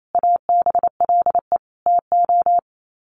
Morse code, heard as a keyed tone, about 28 wpm, 700 Hz. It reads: ABLE TO
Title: Sets of 5 Words 28wpm